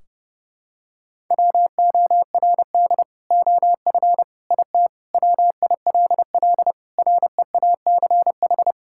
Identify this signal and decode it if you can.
Morse code, heard as a keyed tone, about 30 wpm, 700 Hz. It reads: WORD OF IT WILL REACH
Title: Sentences from Top 1600 Words 30wpm